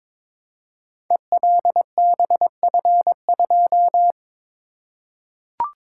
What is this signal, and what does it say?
Morse code, heard as a keyed tone, about 22 wpm, 700 Hz. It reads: ELBF2
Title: Code Groups - 5 Letter:Numbers - 22wpm